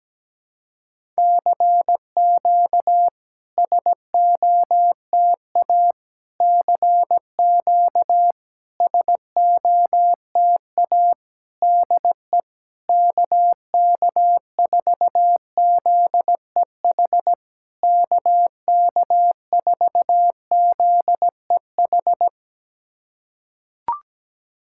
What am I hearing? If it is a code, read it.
Morse code, heard as a keyed tone, about 17 wpm, 690 Hz. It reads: CQ SOTA CQ SOTA DE KK4ZEH KK4ZEH
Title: SOTA QSO Practice 17wpm